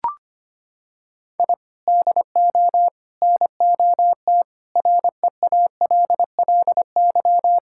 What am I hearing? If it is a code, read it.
Morse code, heard as a keyed tone, about 25 wpm, 700 Hz. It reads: I DO NOT REALLY
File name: practice / 25wpm / Sentences with 4-Word Phrases - 25wpm